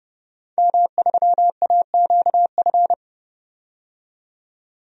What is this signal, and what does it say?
Morse code, heard as a keyed tone, about 30 wpm, 700 Hz. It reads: M3AQF
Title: DXCC-100 Call Signs - 5 Characters - 30wpm